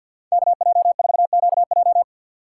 Morse code, heard as a keyed tone, about 53 wpm, 690 Hz. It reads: KJ4QJ
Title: USA Call Signs - 5 Characters - 6x Character Spacing - 50wpm